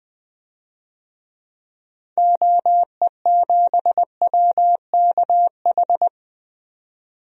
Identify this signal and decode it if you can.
Morse code, heard as a keyed tone, about 21 wpm, 700 Hz. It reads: OE7WKH
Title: DXCC-100 Call Signs - 6 Characters - 20wpm